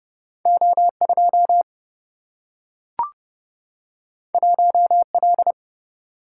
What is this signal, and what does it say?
Morse code, heard as a keyed tone, about 31 wpm, 700 Hz. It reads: O 2 1 L
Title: Single-Letter Single-Number 30wpm